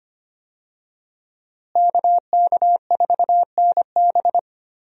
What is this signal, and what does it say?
Morse code, heard as a keyed tone, about 25 wpm, 700 Hz. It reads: KK4NB
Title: USA Call Signs - 5 Characters - Repeated 3x - 25wpm